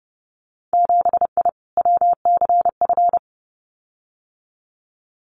Morse code, heard as a keyed tone, about 30 wpm, 700 Hz. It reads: 7I WCF
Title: Winter Field Day Exchanges Encore 30wpm